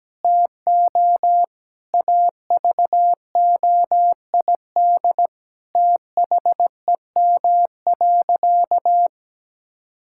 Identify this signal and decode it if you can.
Morse code, heard as a keyed tone, about 17 wpm, 700 Hz. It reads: TO AVOID THEM.
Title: Sentences from Top 1000 Words - Review Part 6 - 17wpm